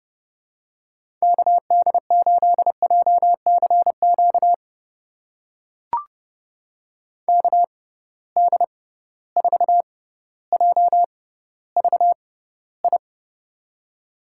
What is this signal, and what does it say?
Morse code, heard as a keyed tone, about 30 wpm, spaced as for 15 wpm, 700 Hz. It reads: KD8JCQ KD4JVI
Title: USA Call Signs - 6 Characters - 6x Character Spacing - 30wpm